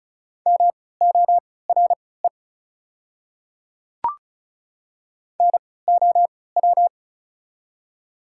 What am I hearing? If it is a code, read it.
Morse code, heard as a keyed tone, about 36 wpm, spaced as for 20 wpm, 690 Hz. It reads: MORE NOW
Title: 3x Character Spacing - Top 500 Words - 35wpm